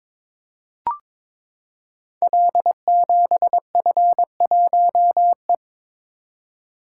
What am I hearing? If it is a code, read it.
Morse code, heard as a keyed tone, about 22 wpm, 700 Hz. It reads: L7F1E
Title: Code Groups - 5 Letter:Numbers - 22wpm